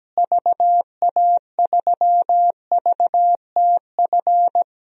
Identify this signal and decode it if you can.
Morse code, heard as a keyed tone, about 17 wpm, 700 Hz. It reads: VA3VTF
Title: DXCC-100 Call Signs - 6 Characters - Repeated 2x - 17wpm